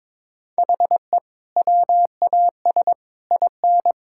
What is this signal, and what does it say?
Morse code, heard as a keyed tone, about 22 wpm, 700 Hz. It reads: HE WAS IN
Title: Sentences from Top 1000 Words - Review Part 5 - 22wpm